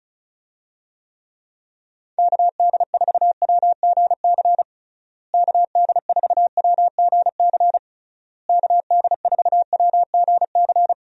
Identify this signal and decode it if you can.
Morse code, heard as a keyed tone, about 36 wpm, 700 Hz. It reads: KD4WGC KD4WGC KD4WGC
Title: USA Call Signs - 6 Characters - Repeated 3x - 35wpm